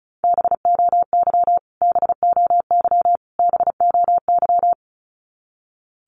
Morse code, heard as a keyed tone, about 34 wpm, 700 Hz. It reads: BOY BOY BOY
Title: Top 500 Words Repeated 35wpm